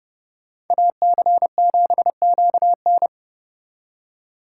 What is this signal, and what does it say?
Morse code, heard as a keyed tone, about 39 wpm, 700 Hz. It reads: AC7QN